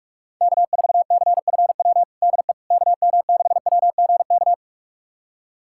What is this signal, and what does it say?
Morse code, heard as a keyed tone, about 46 wpm, 700 Hz. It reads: K4CFW DE KM6WGK
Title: USA Callsign DE Callsign 45wpm